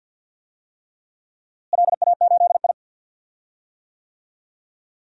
Morse code, heard as a keyed tone, about 47 wpm, 700 Hz. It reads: LA8I
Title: DXCC-100 Call Signs - 4 Characters - Repeated 3x - 50wpm